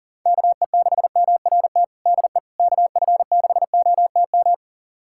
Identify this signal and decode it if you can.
Morse code, heard as a keyed tone, about 40 wpm, 700 Hz. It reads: KE6MRT DE KF6OTM
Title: USA Callsign DE Callsign - 6 Characters - 40wpm